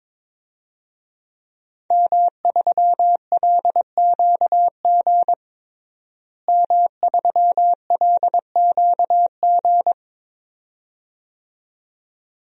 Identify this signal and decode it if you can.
Morse code, heard as a keyed tone, about 22 wpm, 700 Hz. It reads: M3LQG M3LQG
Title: DXCC-100 Call Signs - 5 Characters - Repeated 2x - 22wpm